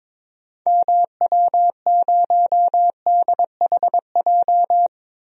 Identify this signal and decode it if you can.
Morse code, heard as a keyed tone, about 22 wpm, 700 Hz. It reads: MW0DHJ
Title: DXCC-100 Call Signs - 6 Characters - 22wpm